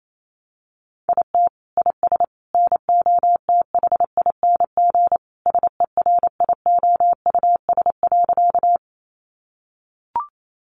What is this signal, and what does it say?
Morse code, heard as a keyed tone, about 28 wpm, 700 Hz. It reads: IT IS NOTHING SERIOUS.
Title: Sentences from Top 1100 Words 28wpm